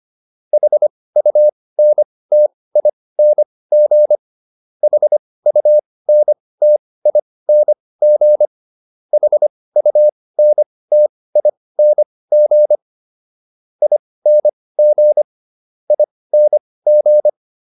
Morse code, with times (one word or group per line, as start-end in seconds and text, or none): 0.5-4.2 HUNTING
4.8-8.5 HUNTING
9.1-12.8 HUNTING
13.8-15.2 ING
15.9-17.3 ING